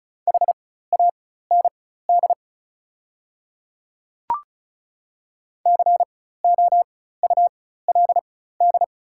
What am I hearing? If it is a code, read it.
Morse code, heard as a keyed tone, about 36 wpm, spaced as for 16 wpm, 700 Hz. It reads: HAND COULD
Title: 4x Character Spacing - Top 200 Words - 35wpm